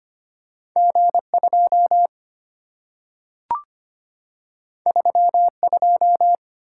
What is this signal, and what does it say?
Morse code, heard as a keyed tone, about 25 wpm, 700 Hz. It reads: G2 32
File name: Two Letter-Number Combinations - 25wpm